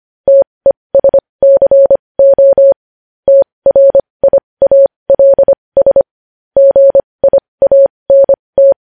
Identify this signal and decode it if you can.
Morse code, heard as a keyed tone, about 25 wpm, spaced as for 20 wpm, 550 Hz. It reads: TESCO TRIALS GIANT
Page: news-headlines-in-morse-code-at-20-wpm